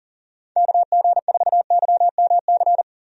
Morse code, heard as a keyed tone, about 40 wpm, 700 Hz. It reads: KG4YMC
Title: USA Call Signs - 6 Characters - Repeated 2x - 40wpm